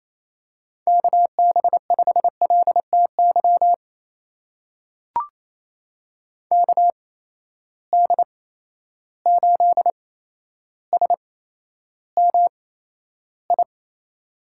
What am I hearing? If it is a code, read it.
Morse code, heard as a keyed tone, about 28 wpm, spaced as for 12 wpm, 700 Hz. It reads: KB5LTY KD8SMI
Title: USA Call Signs - 6 Characters - 8x Character Spacing - 28wpm